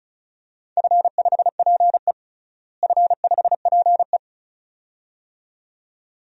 Morse code, heard as a keyed tone, about 36 wpm, 700 Hz. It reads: F5PE F5PE
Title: DXCC-100 Call Signs - 4 Characters - Repeated 2x - 35wpm